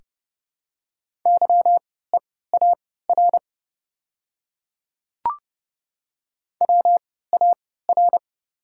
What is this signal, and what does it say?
Morse code, heard as a keyed tone, about 32 wpm, spaced as for 17 wpm, 690 Hz. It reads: YEAR WAR
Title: 3x Character Spacing - Top 500 Words - 30wpm